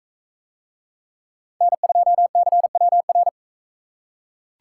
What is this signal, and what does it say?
Morse code, heard as a keyed tone, about 41 wpm, 700 Hz. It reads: N2CWR